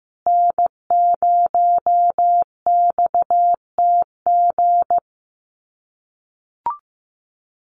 Morse code, heard as a keyed tone, about 15 wpm, 700 Hz. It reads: N0XTG
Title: USA Call Signs Repeated 3x 15wpm